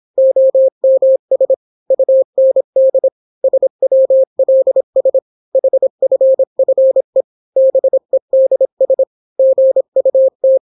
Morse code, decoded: OMS UND SWLS HFFE BEDS GUT